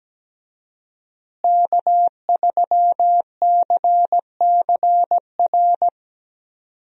Morse code, heard as a keyed tone, about 17 wpm, 700 Hz. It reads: K3CCR
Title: USA Call Signs - 5 Characters - Repeated 3x - 17wpm